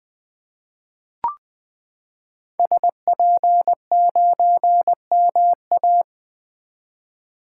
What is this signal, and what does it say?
Morse code, heard as a keyed tone, about 20 wpm, 700 Hz. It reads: SP9MA
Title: 2023 SOTA Activator Call Signs - 20wpm